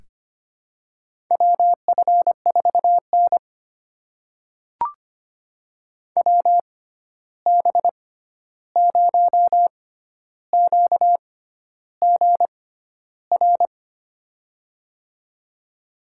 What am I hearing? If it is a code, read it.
Morse code, heard as a keyed tone, about 25 wpm, spaced as for 11 wpm, 700 Hz. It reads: WF4N WB0QGR
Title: USA Call Signs - 6x Character Spacing - 25wpm